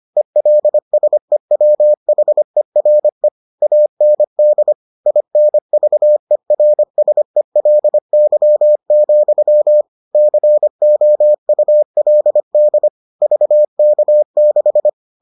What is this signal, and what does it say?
Morse code, heard as a keyed tone, about 25 wpm, 600 Hz. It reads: ELSEWHERE AND INVERSELY, COULD VK6